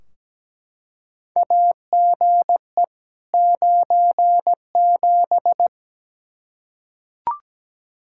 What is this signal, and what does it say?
Morse code, heard as a keyed tone, about 17 wpm, 700 Hz. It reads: AGE 97
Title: Practice QSO - Exchange 3 - More Info - 17wpm